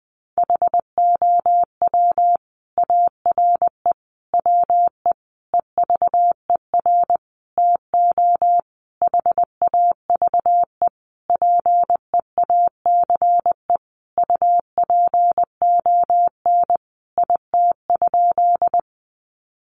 Morse code, heard as a keyed tone, about 20 wpm, 700 Hz. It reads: HOW ARE WE EVER TO HAVE PEACE UPON IT?
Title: Sentences from Top 1500 Words 20wpm